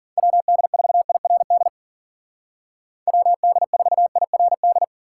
Morse code, heard as a keyed tone, about 53 wpm, 700 Hz. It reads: WD4IRD WD4IRD